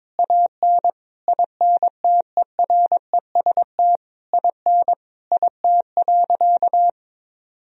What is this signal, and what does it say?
Morse code, heard as a keyed tone, about 22 wpm, 700 Hz. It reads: AN INTEREST IN IT.
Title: Sentences from Top 1000 Words - Review Part 3 - 22wpm